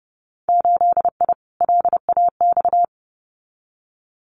Morse code, heard as a keyed tone, about 31 wpm, 700 Hz. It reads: 8I LAX
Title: Winter Field Day Exchanges Encore 30wpm